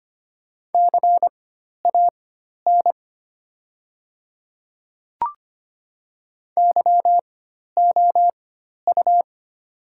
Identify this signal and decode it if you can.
Morse code, heard as a keyed tone, about 25 wpm, spaced as for 12 wpm, 700 Hz. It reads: CAN YOU
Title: 4x Character Spacing - Top 100 Words - 25wpm